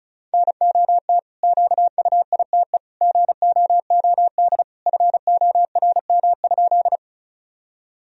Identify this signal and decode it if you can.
Morse code, heard as a keyed tone, about 36 wpm, 700 Hz. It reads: NOT QUITE GOOD FORM?
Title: Sentences from Top 1000 Words - Review Part 1 - 35wpm